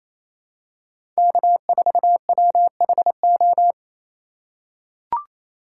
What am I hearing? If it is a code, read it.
Morse code, heard as a keyed tone, about 28 wpm, 700 Hz. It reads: K4WHO
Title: USA Call Signs - 5 Characters - 2x Character Spacing - 28wpm